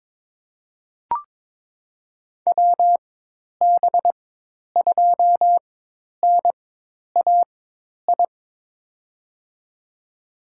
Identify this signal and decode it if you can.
Morse code, heard as a keyed tone, about 22 wpm, spaced as for 10 wpm, 700 Hz. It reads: WB2NAI
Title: USA Call Signs - 6 Characters - 4x Character Spacing - 22wpm